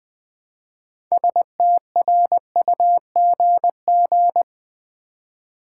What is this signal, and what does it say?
Morse code, heard as a keyed tone, about 20 wpm, 700 Hz. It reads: STRUGG
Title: Build A Word - 2 Letters At A Time - 8-Letter Words - 20wpm